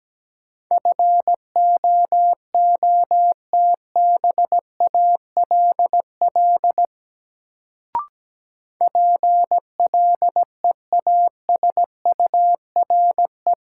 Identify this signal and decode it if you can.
Morse code, heard as a keyed tone, about 17 wpm, 700 Hz. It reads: FOOTBALL PLEASURE
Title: Sentences from Top 1600 Words - New Word Drill - 17wpm